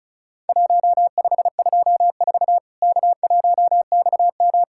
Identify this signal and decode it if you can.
Morse code, heard as a keyed tone, about 36 wpm, 700 Hz. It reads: 1524 K1XM
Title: Worked All Europe DX Contest - QTC Exchange - 35wpm